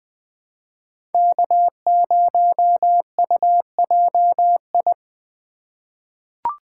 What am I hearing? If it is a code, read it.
Morse code, heard as a keyed tone, about 20 wpm, 700 Hz. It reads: K0UJI
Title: USA Call Signs - 5 Characters - 2x Character Spacing - 20wpm